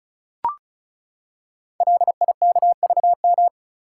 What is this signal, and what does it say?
Morse code, heard as a keyed tone, about 36 wpm, 700 Hz. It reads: LIKVM
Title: Code Groups - 5 Letters - 35wpm